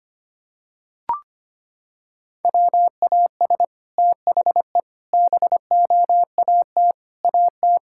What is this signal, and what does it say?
Morse code, heard as a keyed tone, about 25 wpm, 700 Hz. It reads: WAS THE BOAT AT
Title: Sentences from Top 1000 Words - Review Part 5 - 25wpm